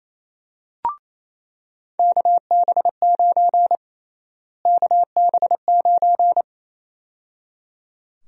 Morse code, heard as a keyed tone, about 27 wpm, 700 Hz. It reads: KB9 KB9
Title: USA Call Sign Prefixes - Repeated 2x - 28wpm